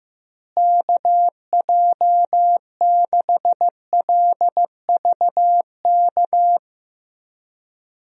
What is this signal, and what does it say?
Morse code, heard as a keyed tone, about 15 wpm, 700 Hz. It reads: KJ6LVK
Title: USA Call Signs - 6 Characters - Repeated 3x - 15wpm